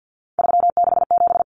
Morse code, heard as a keyed tone, about 48 wpm, 700 Hz. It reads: 367
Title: Common Cut Numbers - 3-Digit 50wpm